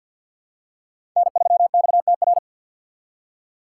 Morse code, heard as a keyed tone, about 47 wpm, 700 Hz. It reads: N3XTR
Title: USA Call Signs - 5 Characters - 8x Character Spacing - 50wpm